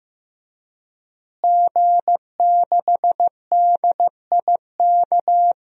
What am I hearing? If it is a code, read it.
Morse code, heard as a keyed tone, about 15 wpm, 700 Hz. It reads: G6DIK
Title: DXCC-100 Call Signs - 5 Characters - Repeated 3x - 15wpm